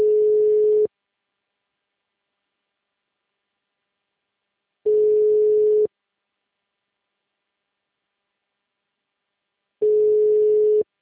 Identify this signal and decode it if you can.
Ringback tone (Europe, 425 Hz).
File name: out- -2023- - - .0